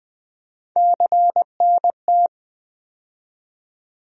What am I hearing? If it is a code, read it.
Morse code, heard as a keyed tone, about 20 wpm, 700 Hz. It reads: CNT
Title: USA Call Sign Postfixes - 3 Letters 20wpm